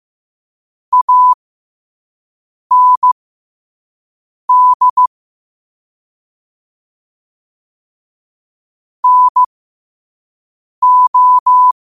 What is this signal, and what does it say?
Morse code, heard as a keyed tone, about 15 wpm, spaced as for 5 wpm, 1000 Hz. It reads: AND NO
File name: 5wpm